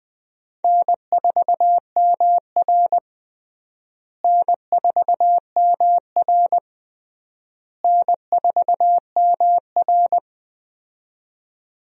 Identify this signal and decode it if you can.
Morse code, heard as a keyed tone, about 20 wpm, 700 Hz. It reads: N4MR N4MR N4MR
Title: USA Call Signs - 4 Characters - Repeated 3x - 20wpm